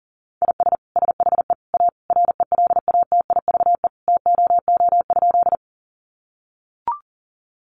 Morse code, heard as a keyed tone, about 40 wpm, 700 Hz. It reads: IS SHE A RELATIVE TOO?
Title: Sentences from Top 1400 Words 40wpm